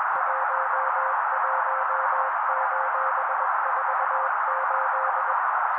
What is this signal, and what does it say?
Morse code, heard as a keyed tone, about 21 wpm, 530 Hz. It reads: 11:48